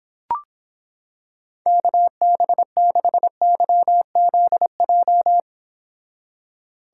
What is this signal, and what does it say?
Morse code, heard as a keyed tone, about 26 wpm, 700 Hz. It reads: KB6YZJ